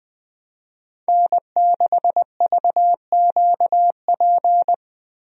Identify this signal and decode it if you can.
Morse code, heard as a keyed tone, about 20 wpm, 700 Hz. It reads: N6VQP
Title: USA Call Signs - 5 Characters - Repeated 3x - 20wpm